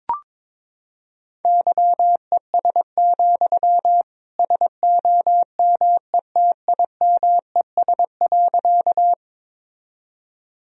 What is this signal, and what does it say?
Morse code, heard as a keyed tone, about 21 wpm, 700 Hz. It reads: YES, SOMETIMES.
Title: Sentences from Top 1000 Words - Review Part 3 - 22wpm